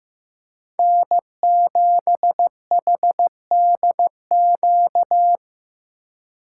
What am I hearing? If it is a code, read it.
Morse code, heard as a keyed tone, about 15 wpm, 700 Hz. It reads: N7HDQ
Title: USA Call Signs - 2x Character Spacing - 15wpm